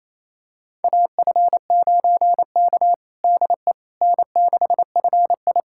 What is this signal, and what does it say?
Morse code, heard as a keyed tone, about 28 wpm, 700 Hz. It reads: AF9K DE N6FI
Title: USA Callsign DE Callsign - 4 Characters - 28wpm